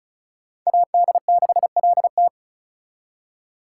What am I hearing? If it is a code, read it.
Morse code, heard as a keyed tone, about 44 wpm, 700 Hz. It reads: AD6LT